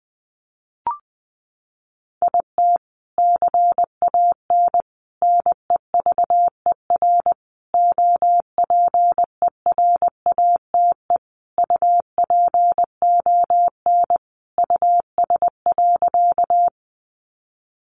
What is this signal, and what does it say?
Morse code, heard as a keyed tone, about 20 wpm, 700 Hz. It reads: IT CAN NEVER OPERATE UPON US.
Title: Sentences from Top 1500 Words 20wpm